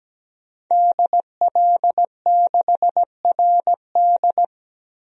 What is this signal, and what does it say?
Morse code, heard as a keyed tone, about 17 wpm, 700 Hz. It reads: DL6RD
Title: DXCC-100 Call Signs - 5 Characters - Repeated 2x - 17wpm